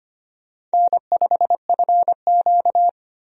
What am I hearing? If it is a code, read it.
Morse code, heard as a keyed tone, about 25 wpm, 700 Hz. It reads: N5FQ
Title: USA Call Signs - 4 Characters - 25wpm